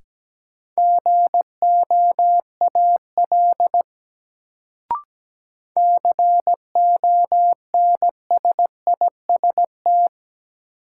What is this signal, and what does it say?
Morse code, heard as a keyed tone, about 17 wpm, 700 Hz. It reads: GOAL CONSIST
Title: Sentences from Top 1400 Words - New Word Drill Alt - 17wpm